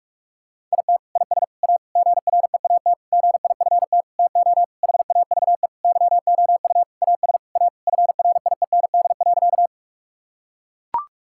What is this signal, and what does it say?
Morse code, heard as a keyed tone, about 41 wpm, 700 Hz. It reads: IT IS A GREAT GIFT TO HAVE YOU AS A FRIEND.
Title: Sentences from Top 1000 Words - Review Part 6 - 45wpm